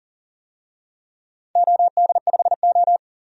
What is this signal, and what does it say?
Morse code, heard as a keyed tone, about 43 wpm, 700 Hz. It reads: OD5O
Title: DXCC-100 Call Signs - 4 Characters - 40wpm